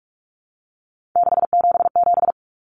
Morse code, heard as a keyed tone, about 44 wpm, 700 Hz. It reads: 677
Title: Numbers - 3-Digit 45wpm